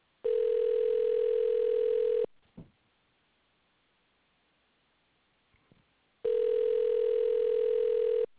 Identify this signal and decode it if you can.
Ringback tone (North America), 440 + 480 Hz.